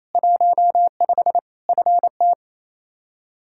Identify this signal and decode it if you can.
Morse code, heard as a keyed tone, about 28 wpm, 700 Hz. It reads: 15 FT
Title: Practice QSO - Exchange 2 - Info - 28wpm